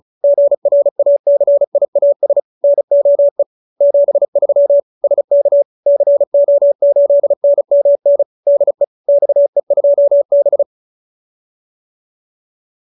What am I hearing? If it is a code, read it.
Morse code, heard as a keyed tone, about 34 wpm, 570 Hz. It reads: GRACIAS NOE 73 SK CO8NMN DE XE2B